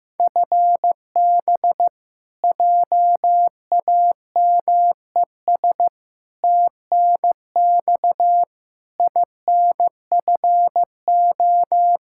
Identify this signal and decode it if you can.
Morse code, heard as a keyed tone, about 15 wpm, 700 Hz. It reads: FB JAMES TNX INFO
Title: Practice QSO - Exchange 3 - More Info - 15wpm